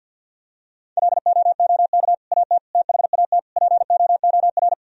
Morse code, heard as a keyed tone, about 50 wpm, 700 Hz. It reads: LOOK AT THAT POOR
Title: Sentences from Top 1000 Words - Review Part 2 - 50wpm